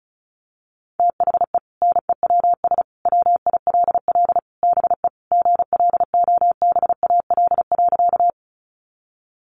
Morse code, heard as a keyed tone, about 34 wpm, 700 Hz. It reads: THE NEWS WILL BE GLOBAL.
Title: Sentences from Top 1200 Words 35wpm